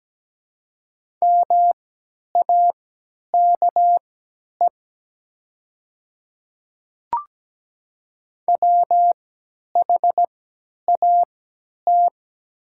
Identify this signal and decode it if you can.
Morse code, heard as a keyed tone, about 17 wpm, spaced as for 10 wpm, 700 Hz. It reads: MAKE WHAT
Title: 3x Character Spacing - Top 100 Words - 17wpm